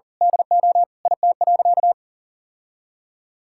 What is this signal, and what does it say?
Morse code, heard as a keyed tone, about 40 wpm, 700 Hz. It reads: DO IT.
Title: Sentences from Top 1600 Words 40wpm